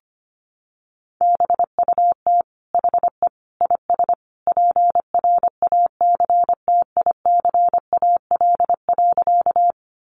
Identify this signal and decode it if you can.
Morse code, heard as a keyed tone, about 25 wpm, 700 Hz. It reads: BUT HE IS PRACTICAL.
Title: Sentences from Top 1400 Words 25wpm